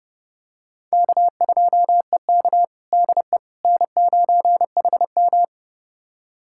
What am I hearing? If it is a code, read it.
Morse code, heard as a keyed tone, about 32 wpm, 700 Hz. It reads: K2EK DE N9HM
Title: USA Callsign DE Callsign - 4 Characters - 30wpm